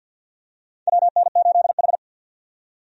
Morse code, heard as a keyed tone, about 52 wpm, 700 Hz. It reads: WN8H
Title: USA Call Signs - 4 Characters - 50wpm